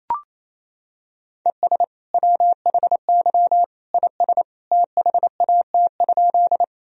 Morse code, heard as a keyed tone, about 27 wpm, 700 Hz. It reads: ES WHY IS THAT?
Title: Sentences from Top 200 Words and Abbreviations 28wpm